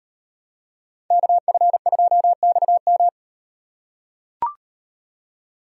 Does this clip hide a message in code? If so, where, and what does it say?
Yes: Morse code from 1.1 to 3.1 s, heard as a keyed tone, about 36 wpm, 700 Hz. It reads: KF2XM